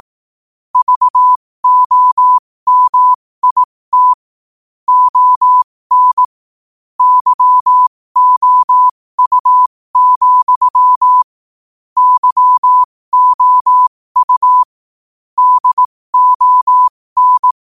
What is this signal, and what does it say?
Morse code, heard as a keyed tone, about 18 wpm, spaced as for 15 wpm, 1000 Hz. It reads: VOMIT ON YOU, YOU DON
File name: qotdfiles / 15wpm